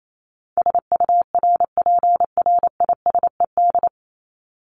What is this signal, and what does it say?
Morse code, heard as a keyed tone, about 28 wpm, 700 Hz. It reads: SURPRISED
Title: Top 1501-1750 Words 28wpm